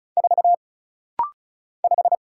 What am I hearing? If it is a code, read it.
Morse code, heard as a keyed tone, about 34 wpm, 700 Hz. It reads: 4 5
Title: Single Letter-Number Alt - Rapid-Fire 35wpm